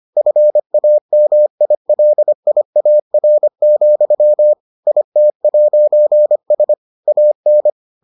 Morse code, heard as a keyed tone, about 25 wpm, 600 Hz. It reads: FAMILIAR, IT'S AN